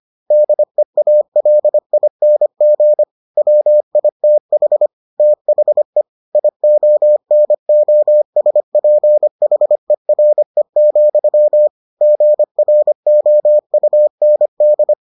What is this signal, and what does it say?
Morse code, heard as a keyed tone, about 25 wpm, 600 Hz. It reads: DEALING WITH THE IONOSPHERE, GROUND